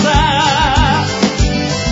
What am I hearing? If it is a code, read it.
There is no code.